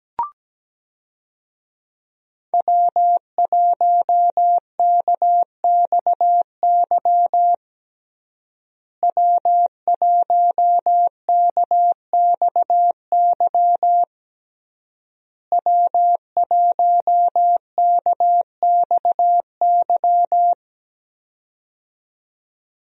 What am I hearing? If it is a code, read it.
Morse code, heard as a keyed tone, about 17 wpm, 700 Hz. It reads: W1KXY W1KXY W1KXY
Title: USA Call Signs - 5 Characters - Repeated 3x - 17wpm